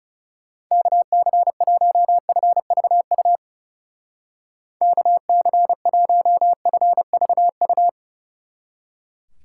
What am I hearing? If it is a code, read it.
Morse code, heard as a keyed tone, about 36 wpm, 700 Hz. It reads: KC1FVU KC1FVU
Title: Speed Racing - USA 2x3 Call Signs 30wpm